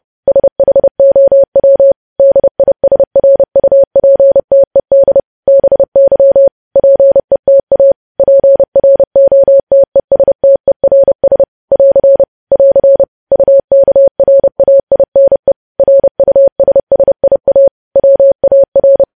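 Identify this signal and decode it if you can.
Morse code, heard as a keyed tone, about 30 wpm, 550 Hz. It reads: SHOW DISRUPTED BY PETA PROTESTERS <AR> <AR> UKRAINE RUSSIA WAR
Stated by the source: Morse code practice of news headlines at 30 wpm, preceded by vvv.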